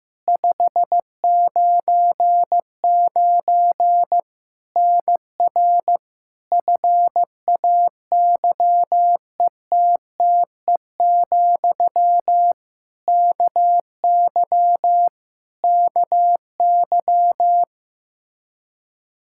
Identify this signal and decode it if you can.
Morse code, heard as a keyed tone, about 16 wpm, 700 Hz. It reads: 599 NR FAYETTE, KY KY
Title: Practice QSO - Exchange 1 - Beginning - 15wpm